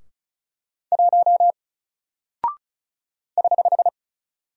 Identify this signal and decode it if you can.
Morse code, heard as a keyed tone, about 36 wpm, 700 Hz. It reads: 1 <HH>
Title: Single Character (Letter:Number:Punctuation:Prosign) Alt 35wpm